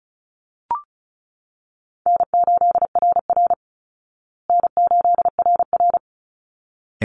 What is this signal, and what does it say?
Morse code, heard as a keyed tone, about 34 wpm, 700 Hz. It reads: N8RR N8RR
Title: Easy Call Signs Repeated 2x - 35wpm